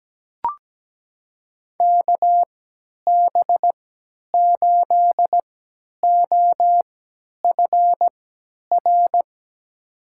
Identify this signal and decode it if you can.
Morse code, heard as a keyed tone, about 17 wpm, spaced as for 10 wpm, 700 Hz. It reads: KB8OFR
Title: USA Call Signs - 6 Characters - 3x Character Spacing - 17wpm